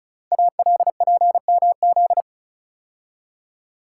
Morse code, heard as a keyed tone, about 36 wpm, 700 Hz. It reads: ALPMZ
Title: Code Groups - 5 Letter:Numbers - 35wpm